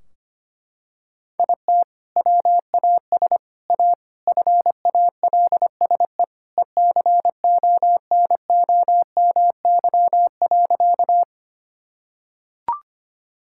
Morse code, heard as a keyed tone, about 25 wpm, 700 Hz. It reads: IT WAS A FALSE ECONOMY.
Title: Sentences from Top 1600 Words 25wpm